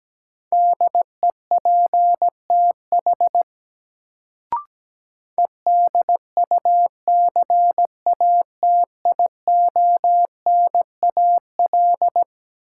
Morse code, heard as a keyed tone, about 17 wpm, 700 Hz. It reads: DEPTH EDUCATIONAL
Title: Sentences from Top 1200 Words - New Word Drill - 17wpm